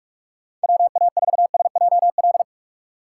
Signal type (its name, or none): Morse code